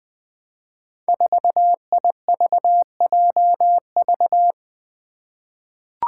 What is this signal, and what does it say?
Morse code, heard as a keyed tone, about 20 wpm, 700 Hz. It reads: 4IVJV
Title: Five Challenging Letter-Number Combinations 20wpm